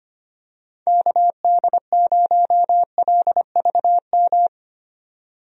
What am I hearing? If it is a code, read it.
Morse code, heard as a keyed tone, about 25 wpm, 700 Hz. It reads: KD0LVM
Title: USA Call Signs - 6 Characters - 25wpm